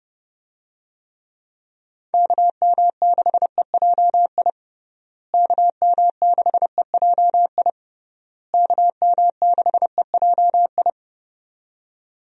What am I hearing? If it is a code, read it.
Morse code, heard as a keyed tone, about 31 wpm, 700 Hz. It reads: KM6EJI KM6EJI KM6EJI
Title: USA Call Signs - 6 Characters - Repeated 3x - 30wpm